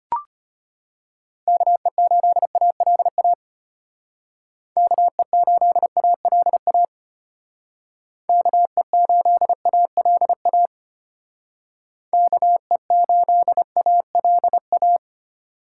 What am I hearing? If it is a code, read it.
Morse code, heard as a keyed tone, about 32 wpm, 700 Hz. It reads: KE8ALA KE8ALA KE8ALA KE8ALA